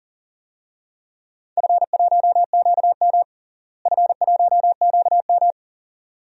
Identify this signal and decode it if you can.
Morse code, heard as a keyed tone, about 40 wpm, 690 Hz. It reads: F1QM F1QM
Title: DXCC-100 Call Signs - 4 Characters - Repeated 2x - 40wpm